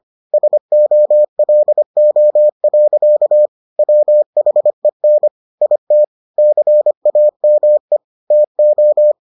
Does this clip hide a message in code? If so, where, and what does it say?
Yes: Morse code from 0.3 to 9.1 s, heard as a keyed tone, about 25 wpm, 600 Hz. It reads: SOLO. WHEN IT CAME TO